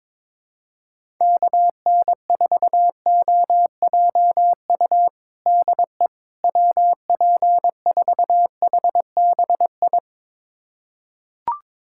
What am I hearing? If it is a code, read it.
Morse code, heard as a keyed tone, about 22 wpm, 700 Hz. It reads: KN4OJU DE WP4HBI
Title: USA Callsign DE Callsign 22wpm